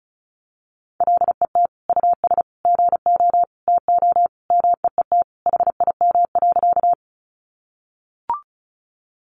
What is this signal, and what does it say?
Morse code, heard as a keyed tone, about 34 wpm, 700 Hz. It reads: LET US GO TO MEET HIM.
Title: Sentences from Top 1100 Words 35wpm